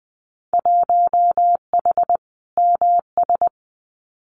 Morse code, heard as a keyed tone, about 20 wpm, 700 Hz. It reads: 1H MS
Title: Winter Field Day Exchanges 20wpm